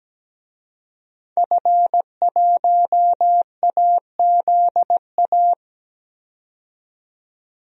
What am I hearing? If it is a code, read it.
Morse code, heard as a keyed tone, about 17 wpm, 700 Hz. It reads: F1AZA
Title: DXCC-100 Call Signs - 5 Characters - 17wpm